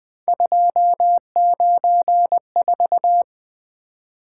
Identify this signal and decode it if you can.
Morse code, heard as a keyed tone, about 20 wpm, 700 Hz. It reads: 294
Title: Three Letter-Number Combinations - 20wpm